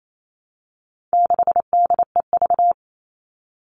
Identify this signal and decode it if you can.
Morse code, heard as a keyed tone, about 28 wpm, 700 Hz. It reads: 6DEV
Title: All Cut Numbers - 4-Digit 28wpm